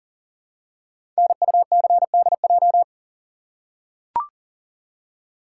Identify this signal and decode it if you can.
Morse code, heard as a keyed tone, about 42 wpm, 690 Hz. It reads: NUCDJ
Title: Code Groups - 5 Letters - 40wpm